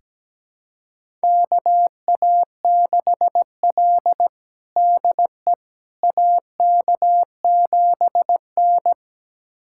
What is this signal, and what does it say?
Morse code, heard as a keyed tone, about 17 wpm, 700 Hz. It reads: KA6L DE AK7N
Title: USA Callsign DE Callsign - 4 Characters - 17wpm